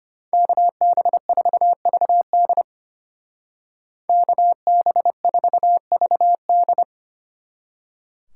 Morse code, heard as a keyed tone, about 31 wpm, 700 Hz. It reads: KB4VD KB4VD
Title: Speed Racing - USA 2x2 Call Signs 25wpm